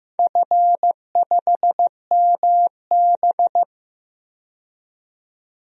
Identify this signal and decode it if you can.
Morse code, heard as a keyed tone, about 15 wpm, 700 Hz. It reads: F5MB
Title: DXCC-100 Call Signs - 4 Characters - 15wpm